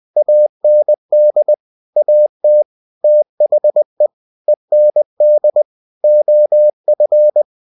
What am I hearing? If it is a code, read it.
Morse code, heard as a keyed tone, about 20 wpm, 600 Hz. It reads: AND AT THE END OF
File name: cw-books / winnie-the-pooh / winnie-20wpm-page-0019